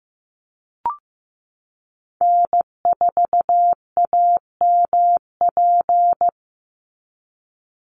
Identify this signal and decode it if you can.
Morse code, heard as a keyed tone, about 15 wpm, 700 Hz. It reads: N4AMP
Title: USA Call Signs 15wpm